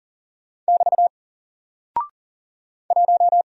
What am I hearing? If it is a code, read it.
Morse code, heard as a keyed tone, about 40 wpm, 700 Hz. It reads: <BT> 1
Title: Single Character (Letter:Number:Punctuation:Prosign) Alt 40wpm